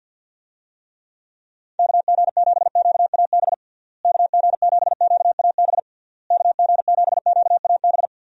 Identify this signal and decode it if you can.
Morse code, heard as a keyed tone, about 50 wpm, 700 Hz. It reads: KG7QAB KG7QAB KG7QAB
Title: USA Call Signs - 6 Characters - Repeated 3x - 50wpm